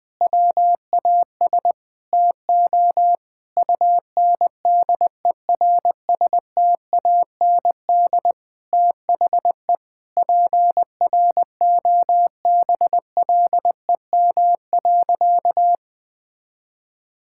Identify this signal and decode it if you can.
Morse code, heard as a keyed tone, about 20 wpm, 700 Hz. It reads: WAS TO UNDERSTAND THE PROBLEM.
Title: Sentences with 7-Word Phrases - 20wpm